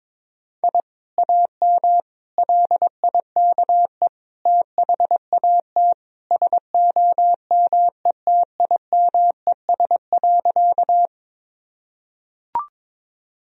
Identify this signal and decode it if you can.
Morse code, heard as a keyed tone, about 21 wpm, 700 Hz. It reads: I AM LIKE THAT SOMETIMES.
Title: Sentences from Top 1000 Words - Review Part 3 - 22wpm